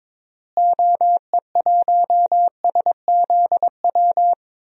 Morse code, heard as a keyed tone, about 22 wpm, 700 Hz. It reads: OE1SZW
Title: DXCC-100 Call Signs - 6 Characters - Repeated 3x - 22wpm